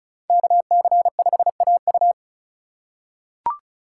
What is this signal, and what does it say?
Morse code, heard as a keyed tone, about 36 wpm, 700 Hz. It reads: KC5AU
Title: USA Call Signs - 5 Characters - 2x Character Spacing - 35wpm